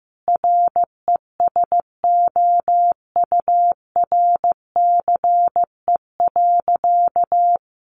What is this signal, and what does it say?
Morse code, heard as a keyed tone, about 15 wpm, 700 Hz. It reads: RESOURCE.
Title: Sentences from Top 1300 Words 15wpm